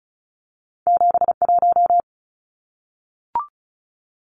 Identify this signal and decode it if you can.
Morse code, heard as a keyed tone, about 36 wpm, 700 Hz. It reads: 71
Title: Numbers - 2-Digit 35wpm